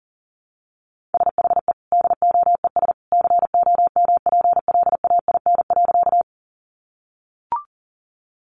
Morse code, heard as a keyed tone, about 37 wpm, 700 Hz. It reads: SHE DOES COMPLAIN.
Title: Sentences from Top 1300 Words 40wpm